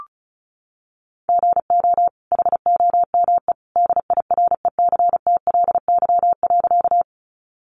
Morse code, heard as a keyed tone, about 34 wpm, 700 Hz. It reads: GO HOME DIRECTLY.
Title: Sentences from Top 800 Words 35wpm